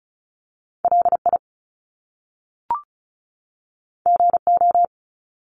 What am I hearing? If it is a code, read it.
Morse code, heard as a keyed tone, about 36 wpm, 700 Hz. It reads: LI GO
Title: Top Two Letter Combinations in Words 35wpm